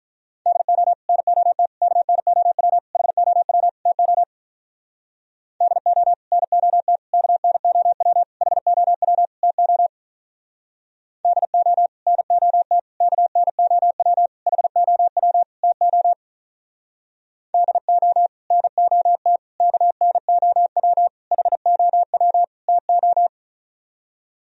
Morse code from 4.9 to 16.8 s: DO NOT KNOW HOW TO DO NOT KNOW HOW TO

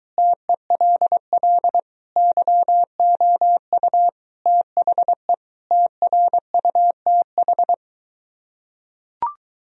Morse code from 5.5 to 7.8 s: TRUTH

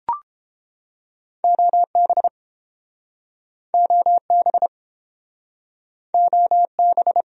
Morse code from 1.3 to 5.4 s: OB OB

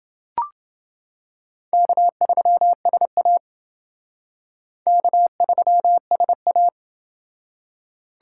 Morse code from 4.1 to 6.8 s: K3SA

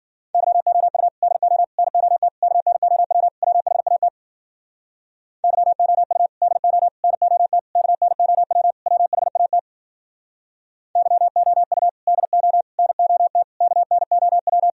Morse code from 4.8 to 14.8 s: YOU DO NOT KNOW WHAT YOU DO NOT KNOW

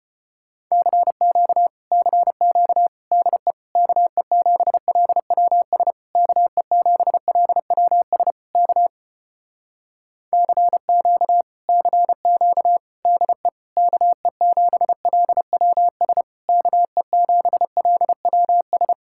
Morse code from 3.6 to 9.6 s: KE7LWS KE7LWS K

